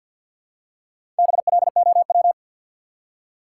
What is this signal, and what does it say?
Morse code, heard as a keyed tone, about 47 wpm, 700 Hz. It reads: BLOW
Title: Four Letter Words - 50wpm